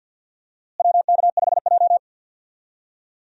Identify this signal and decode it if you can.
Morse code, heard as a keyed tone, about 48 wpm, 700 Hz. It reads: WK5J
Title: USA Call Signs - 4 Characters - Repeated 3x - 50wpm